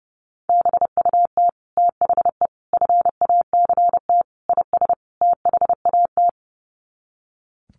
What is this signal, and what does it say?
Morse code, heard as a keyed tone, about 30 wpm, 700 Hz. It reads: BUT THE FACT IS THAT
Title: Sets of 5 Words Encore2 30wpm